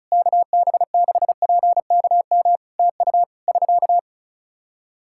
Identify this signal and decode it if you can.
Morse code, heard as a keyed tone, about 36 wpm, 700 Hz. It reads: KB6PKM TU <SK>
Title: Practice QSO - Exchange 4 - Ending - 35wpm